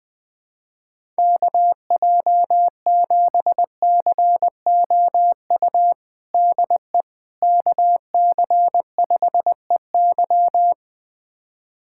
Morse code, heard as a keyed tone, about 21 wpm, 700 Hz. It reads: KJ7COU DE KC5EY
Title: USA Callsign DE Callsign 20wpm